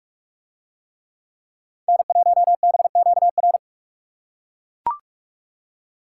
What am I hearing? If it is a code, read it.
Morse code, heard as a keyed tone, about 41 wpm, 700 Hz. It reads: N1BQR